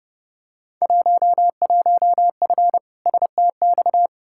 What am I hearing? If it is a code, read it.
Morse code, heard as a keyed tone, about 32 wpm, 700 Hz. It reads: 11F STX
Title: Field Day Exchanges Encore - 30wpm